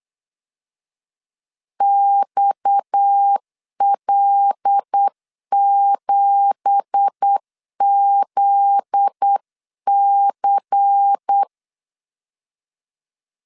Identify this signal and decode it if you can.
Morse code, heard as a keyed tone, about 8 wpm, 800 Hz. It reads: XL7ZC
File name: LEZIONI / 10 SIGNS / 10A SLASH 8wpm 3452